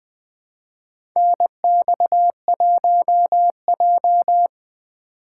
Morse code, heard as a keyed tone, about 20 wpm, 700 Hz. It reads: NX1J
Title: USA Call Signs - 2x Character Spacing - 20wpm